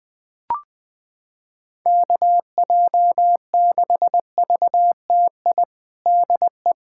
Morse code, heard as a keyed tone, about 19 wpm, 700 Hz. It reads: KJ6VTI DE
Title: USA Callsign DE Callsign - 6 Characters - 20wpm